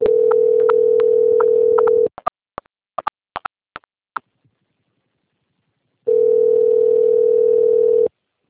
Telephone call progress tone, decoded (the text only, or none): ringback tone